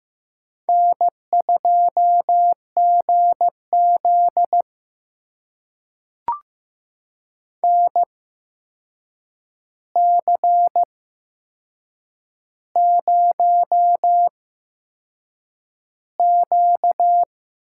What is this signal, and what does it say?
Morse code, heard as a keyed tone, about 15 wpm, spaced as for 6 wpm, 700 Hz. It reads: N2GZ NC0Q
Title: USA Call Signs - 4 Characters - 8x Character Spacing - 15wpm